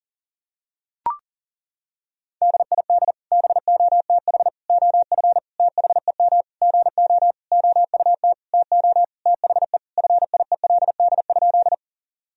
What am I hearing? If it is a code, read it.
Morse code, heard as a keyed tone, about 40 wpm, 700 Hz. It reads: DID BOTH OF THEM GO OUT TO THE FIELD?
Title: Sentences from Top 1000 Words - Review Part 2 - 40wpm